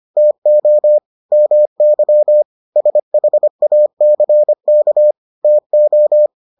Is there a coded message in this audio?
Yes: Morse code from 0.2 to 6.3 s, heard as a keyed tone, about 25 wpm, 600 Hz. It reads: TO MY SHACK TO